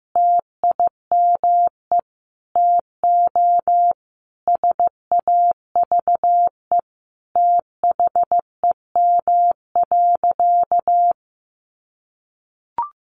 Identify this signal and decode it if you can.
Morse code, heard as a keyed tone, about 15 wpm, 700 Hz. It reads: TIME TO SAVE THEM.
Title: Sentences from Top 1100 Words 15wpm